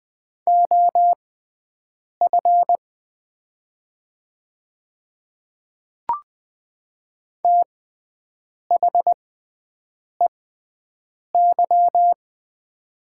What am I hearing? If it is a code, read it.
Morse code, heard as a keyed tone, about 20 wpm, spaced as for 7 wpm, 700 Hz. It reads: OF THEY
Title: 6x Character Spacing - Top 100 Words - 20wpm